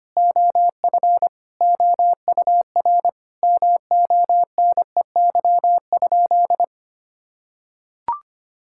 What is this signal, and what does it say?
Morse code, heard as a keyed tone, about 25 wpm, 700 Hz. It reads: OF OUR MONEY?
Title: Sentences from Top 1000 Words - Review Part 5 - 25wpm